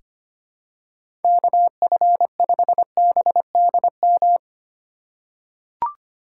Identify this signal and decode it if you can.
Morse code, heard as a keyed tone, about 25 wpm, 700 Hz. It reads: KF5BDM
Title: USA Call Signs - 6 Characters - 3x Character Spacing - 25wpm